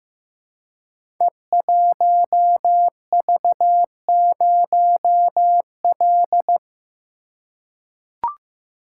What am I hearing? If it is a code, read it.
Morse code, heard as a keyed tone, about 15 wpm, 700 Hz. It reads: E1V0L
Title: Code Groups - 5 Letter:Numbers - 15wpm